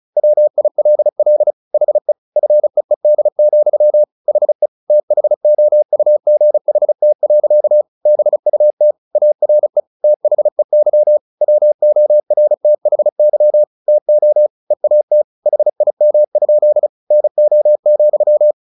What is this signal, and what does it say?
Morse code, heard as a keyed tone, about 35 wpm, 600 Hz. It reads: WILL HE FEED, HE THOUGHT. BUT ARE THEY WORTHY TO EAT HIM? NO,
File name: the old man and the sea 2 35wpm